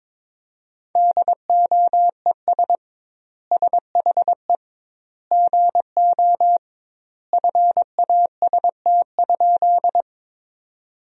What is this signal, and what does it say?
Morse code, heard as a keyed tone, about 22 wpm, 700 Hz. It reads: DOES SHE GO FAST?
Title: 2x Word-Spacing - Sentences from Top 400 Words - 22wpm